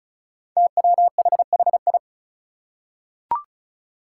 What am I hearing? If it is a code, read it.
Morse code, heard as a keyed tone, about 36 wpm, 700 Hz. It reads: TWHHI
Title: Five Challenging Letter-Number Combinations 35wpm